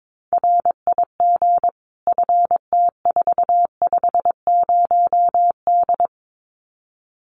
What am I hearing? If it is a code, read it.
Morse code, heard as a keyed tone, about 22 wpm, 700 Hz. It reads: RIG FT450D
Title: Common QSO Phrases 22wpm